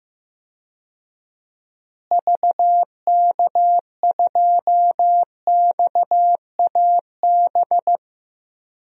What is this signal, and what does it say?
Morse code, heard as a keyed tone, about 15 wpm, 700 Hz. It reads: VK2XAB
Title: DXCC-100 Call Signs - 6 Characters - Repeated 2x - 15wpm